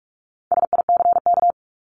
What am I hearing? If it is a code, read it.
Morse code, heard as a keyed tone, about 44 wpm, 700 Hz. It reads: SICK
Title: Top 1501-1750 Words 45wpm